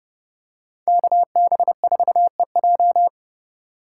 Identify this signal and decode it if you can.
Morse code, heard as a keyed tone, about 29 wpm, 700 Hz. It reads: KB4EJ